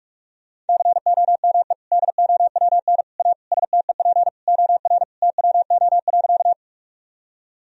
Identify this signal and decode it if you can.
Morse code, heard as a keyed tone, about 41 wpm, 700 Hz. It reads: COME DOWN A STEP OR TWO.
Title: Sentences from Top 1000 Words - Review Part 2 - 45wpm